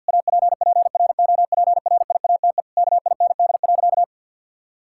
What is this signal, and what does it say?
Morse code, heard as a keyed tone, about 52 wpm, 700 Hz. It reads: APPROPRIATE KIND.
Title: Sentences from Top 1000 Words - Review Part 5 - 50wpm